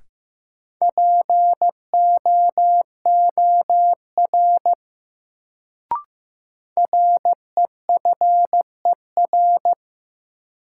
Morse code, heard as a keyed tone, about 15 wpm, 700 Hz. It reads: POOR REFER
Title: Sentences from Top 1000 Words - New Word Drill - 15wpm